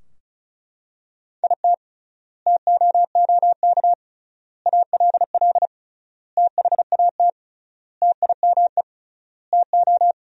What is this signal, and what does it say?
Morse code, heard as a keyed tone, about 36 wpm, 700 Hz. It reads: IT TOOK ALL THAT TIME TO
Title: 3x Word-Spacing - Sentences from Top 300 Words - 35wpm